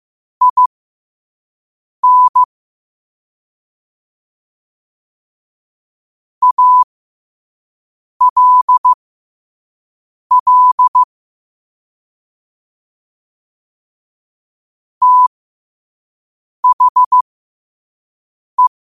Morse code, heard as a keyed tone, about 15 wpm, spaced as for 5 wpm, 1000 Hz. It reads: IN ALL THE
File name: qotdfiles / 5wpm